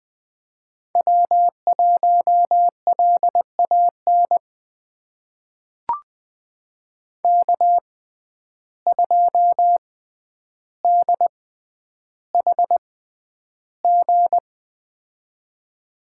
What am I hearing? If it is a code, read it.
Morse code, heard as a keyed tone, about 20 wpm, spaced as for 10 wpm, 700 Hz. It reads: W1LAN K2DHG
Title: USA Call Signs - 5 Characters - 6x Character Spacing - 20wpm